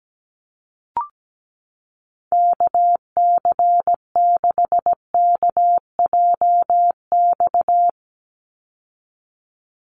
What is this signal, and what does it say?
Morse code, heard as a keyed tone, about 17 wpm, 700 Hz. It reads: KC6KJX
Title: USA Call Signs 17wpm